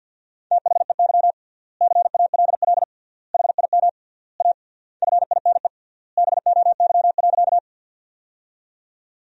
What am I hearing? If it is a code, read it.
Morse code, heard as a keyed tone, about 47 wpm, 700 Hz. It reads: THEY CALL HIM A FINE BOY.
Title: 3x Word-Spacing - Sentences from Top 500 Words - 50wpm